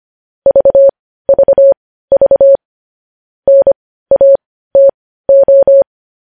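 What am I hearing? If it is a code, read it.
Morse code, heard as a keyed tone, about 25 wpm, spaced as for 15 wpm, 550 Hz. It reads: VVV NATO
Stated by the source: Morse news headlines sent at 15 wpm.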